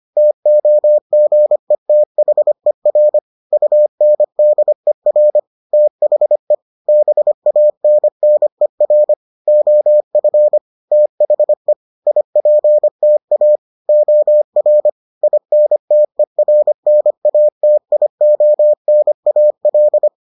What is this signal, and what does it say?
Morse code, heard as a keyed tone, about 25 wpm, 600 Hz. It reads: TOGETHER UNDER THE BANNER OF THE IPTA OR INTERNATIONAL